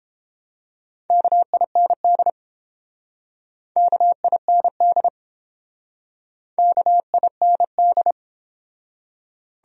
Morse code, heard as a keyed tone, about 29 wpm, 700 Hz. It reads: KIND KIND KIND